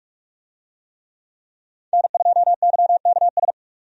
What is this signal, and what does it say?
Morse code, heard as a keyed tone, about 45 wpm, 700 Hz. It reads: N2YKS
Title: USA Call Signs - 5 Characters - 45wpm